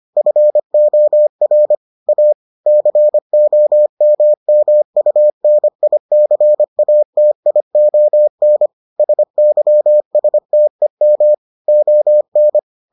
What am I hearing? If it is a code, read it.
Morse code, heard as a keyed tone, about 25 wpm, 600 Hz. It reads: FOR A COMMUNICATION SYSTEM ON